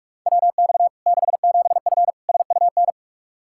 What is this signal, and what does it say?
Morse code, heard as a keyed tone, about 46 wpm, 700 Hz. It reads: WX 67F SUN
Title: K1USN SST QSO Practice 45wpm